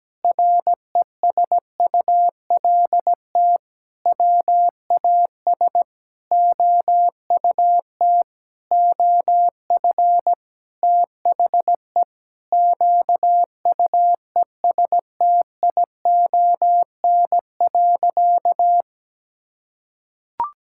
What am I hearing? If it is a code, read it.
Morse code, heard as a keyed tone, about 17 wpm, 700 Hz. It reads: RESULT WAS OUT OF THE QUESTION.
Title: Sentences from Top 1000 Words - Review Part 5 - 17wpm